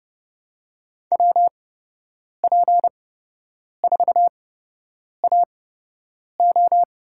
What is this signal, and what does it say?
Morse code, heard as a keyed tone, about 30 wpm, spaced as for 8 wpm, 700 Hz. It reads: WP4AO
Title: USA Call Signs - 8x Character Spacing - 30wpm